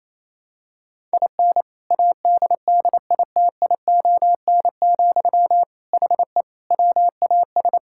Morse code, heard as a keyed tone, about 28 wpm, 700 Hz. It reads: IN ADDITION, HE WAS
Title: Sentences from Top 1000 Words - Review Part 6 - 28wpm